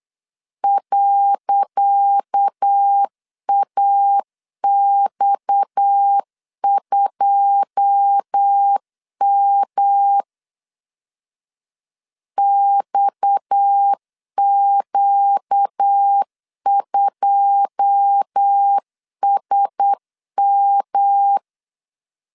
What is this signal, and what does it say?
Morse code, heard as a keyed tone, about 8 wpm, 800 Hz. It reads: .AX2M XQ2SM
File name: LEZIONI / 10 SIGNS / 10C DOT 8wpm 3649